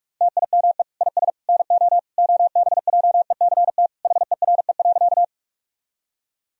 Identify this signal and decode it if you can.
Morse code, heard as a keyed tone, about 46 wpm, 700 Hz. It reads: TIME IS NO OBJECT HERE.
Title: Sentences from Top 1000 Words - Review Part 3 - 45wpm